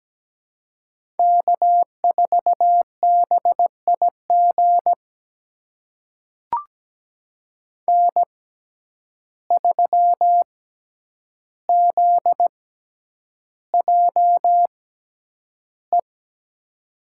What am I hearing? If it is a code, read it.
Morse code, heard as a keyed tone, about 17 wpm, spaced as for 9 wpm, 700 Hz. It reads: K4BIG N3ZJE
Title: USA Call Signs - 5 Characters - 6x Character Spacing - 17wpm